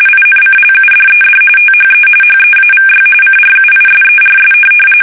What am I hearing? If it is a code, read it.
RTTY, about 45.45 baud. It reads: INCLUDED THE HEAVY CRUISERS
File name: fsk850 / USS-Massachusetts